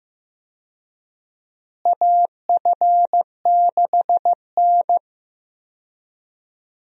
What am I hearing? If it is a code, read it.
Morse code, heard as a keyed tone, about 15 wpm, 700 Hz. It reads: AF6N
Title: USA Call Signs - 4 Characters - Repeated 2x - 15wpm